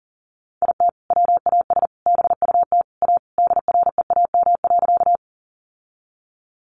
Morse code, heard as a keyed tone, about 40 wpm, 700 Hz. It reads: IT WAS BUT A DREAM.
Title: Sentences from Top 1500 Words 40wpm